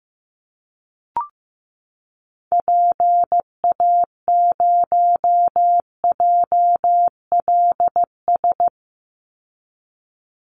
Morse code, heard as a keyed tone, about 15 wpm, 700 Hz. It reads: PA0JLS
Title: Call Signs Repeated 2x - 15wpm